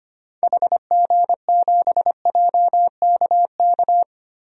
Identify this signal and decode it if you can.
Morse code, heard as a keyed tone, about 25 wpm, 700 Hz. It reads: HG7JKK
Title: DXCC-100 Call Signs - 6 Characters - Repeated 3x - 25wpm